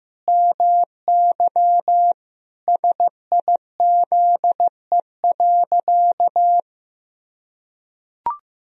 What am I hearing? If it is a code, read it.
Morse code, heard as a keyed tone, about 15 wpm, 700 Hz. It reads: MY SIZE.
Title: Sentences from Top 1000 Words - Review Part 2 - 15wpm